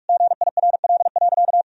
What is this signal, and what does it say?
Morse code, heard as a keyed tone, about 46 wpm, 700 Hz. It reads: GIRL.
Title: Sentences from Top 400 Words and Abbreviations 45wpm